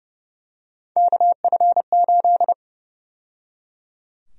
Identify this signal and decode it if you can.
Morse code, heard as a keyed tone, about 30 wpm, 700 Hz. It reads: KF8
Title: USA Call Sign Prefixes - 30wpm